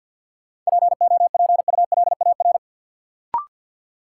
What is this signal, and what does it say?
Morse code, heard as a keyed tone, about 53 wpm, 700 Hz. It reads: POPULAR
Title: Sentences from Top 1000 Words - Review Part 3 - New Word Drill - 50wpm